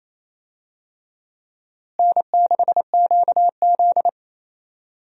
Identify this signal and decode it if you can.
Morse code, heard as a keyed tone, about 28 wpm, 700 Hz. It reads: N6QZ
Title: USA Call Signs - 4 Characters - Repeated 2x - 28wpm